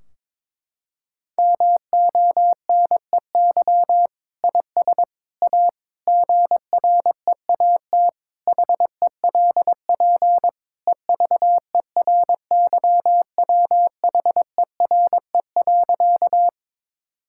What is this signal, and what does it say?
Morse code, heard as a keyed tone, about 22 wpm, 700 Hz. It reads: MONEY IS A GREAT HELP EVERYWHERE.
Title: Sentences from Top 1600 Words 22wpm